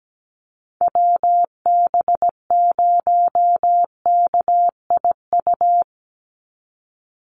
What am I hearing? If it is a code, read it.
Morse code, heard as a keyed tone, about 17 wpm, 700 Hz. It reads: WB0KIU
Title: NA SOTA Chaser Callsigns 17wpm